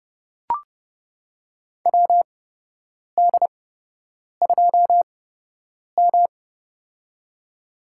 Morse code, heard as a keyed tone, about 31 wpm, spaced as for 8 wpm, 700 Hz. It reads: WD2M
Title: USA Call Signs - 4 Characters - 8x Character Spacing - 30wpm